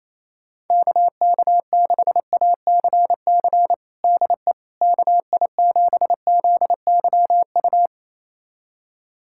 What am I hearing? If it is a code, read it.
Morse code, heard as a keyed tone, about 28 wpm, 700 Hz. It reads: KK6ACC DE KI7ZYU
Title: USA Callsign DE Callsign - 6 Characters - 28wpm